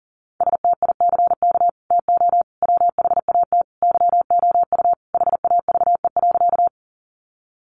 Morse code, heard as a keyed tone, about 40 wpm, 700 Hz. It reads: STICK TO WHAT YOU HAVE.
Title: Sentences from Top 1200 Words 40wpm